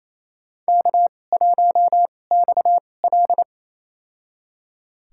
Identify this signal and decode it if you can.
Morse code, heard as a keyed tone, about 27 wpm, spaced as for 20 wpm, 700 Hz. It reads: K1XL
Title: USA Call Signs - 4 Characters - 2x Character Spacing - 28wpm